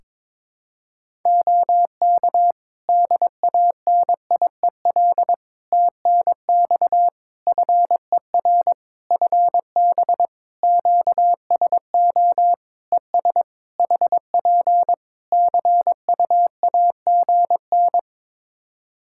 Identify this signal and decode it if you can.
Morse code, heard as a keyed tone, about 22 wpm, 700 Hz. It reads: OK DANIEL TNX FER FB QSO ES HP CUAGN
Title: K1USN SST QSO Practice 22wpm